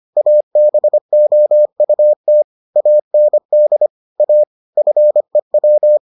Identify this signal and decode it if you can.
Morse code, heard as a keyed tone, about 25 wpm, 600 Hz. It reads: ABOUT AND A FEW